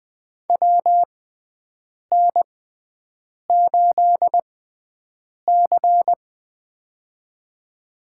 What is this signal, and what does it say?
Morse code, heard as a keyed tone, about 20 wpm, spaced as for 7 wpm, 700 Hz. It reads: WN8C
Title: USA Call Signs - 6x Character Spacing - 20wpm